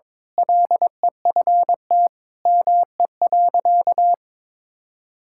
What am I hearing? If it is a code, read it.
Morse code, heard as a keyed tone, about 22 wpm, 700 Hz. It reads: LEFT ME.
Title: Sentences from Top 1600 Words 22wpm